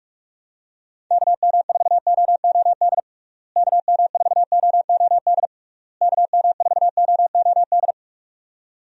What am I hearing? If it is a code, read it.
Morse code, heard as a keyed tone, about 46 wpm, 700 Hz. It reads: KM4OOD KM4OOD KM4OOD
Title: USA Call Signs - 6 Characters - Repeated 3x - 45wpm